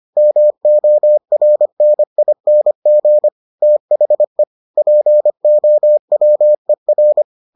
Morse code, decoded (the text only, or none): MORNING THE POWER